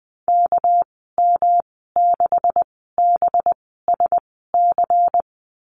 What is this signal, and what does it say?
Morse code, heard as a keyed tone, about 20 wpm, spaced as for 14 wpm, 700 Hz. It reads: KM6BSC
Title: 213 Lesson 34g - Number 8 - Call Signs 20wpm